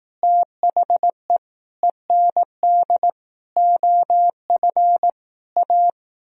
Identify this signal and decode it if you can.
Morse code, heard as a keyed tone, about 18 wpm, 700 Hz. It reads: THE END OF A
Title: Speed-Racing - Sets of 5 Words 15wpm